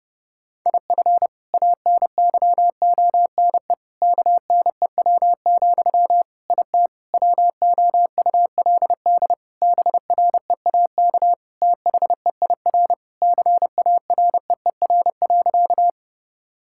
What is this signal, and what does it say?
Morse code, heard as a keyed tone, about 30 wpm, 700 Hz. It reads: IF ANYONE KNEW, IT WOULD BREAK THEIR CAREER.
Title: Sentences from Top 1000 Words - Review Part 6 - 30wpm